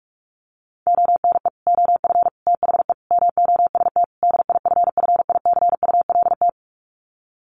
Morse code, heard as a keyed tone, about 44 wpm, 700 Hz. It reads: ONE OF THE MOST DIFFICULT
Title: Sets of 5 Words Encore2 45wpm